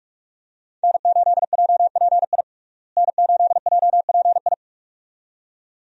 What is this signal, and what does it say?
Morse code, heard as a keyed tone, about 46 wpm, 700 Hz. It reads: N8JPI N8JPI
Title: USA Call Signs - 5 Characters - Repeated 2x - 45wpm